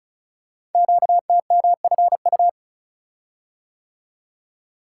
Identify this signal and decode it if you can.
Morse code, heard as a keyed tone, about 36 wpm, 690 Hz. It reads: QTMFU
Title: Code Groups - 5 Letters - 35wpm